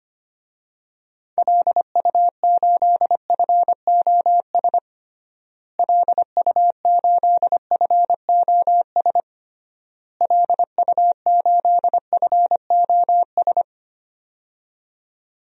Morse code, heard as a keyed tone, about 25 wpm, 700 Hz. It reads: LU8FOS LU8FOS LU8FOS
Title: DXCC-100 Call Signs - 6 Characters - Repeated 3x - 25wpm